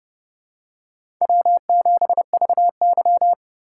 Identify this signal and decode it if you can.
Morse code, heard as a keyed tone, about 30 wpm, 700 Hz. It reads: W7VY
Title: USA Call Signs - 4 Characters - 4x Character Spacing - 30wpm